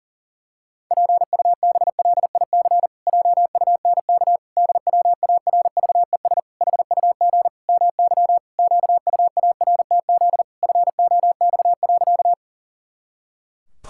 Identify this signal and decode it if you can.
Morse code, heard as a keyed tone, about 37 wpm, 700 Hz. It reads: PUBLIC JUNK DWARVES HUG MY QUARTZ FOX.
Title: Pangram Challenge v2 - 40wpm